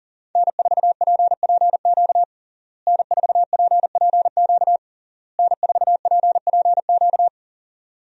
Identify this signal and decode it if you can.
Morse code, heard as a keyed tone, about 37 wpm, 700 Hz. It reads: N4PPQ N4PPQ N4PPQ
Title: USA Call Signs - 5 Characters - Repeated 3x - 40wpm